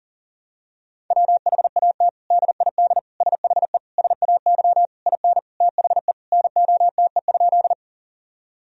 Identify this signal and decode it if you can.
Morse code, heard as a keyed tone, about 40 wpm, 700 Hz. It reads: WHAT DID SHE SAY IN THE NOTE?
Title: Sentences from Top 1000 Words - Review Part 2 - 40wpm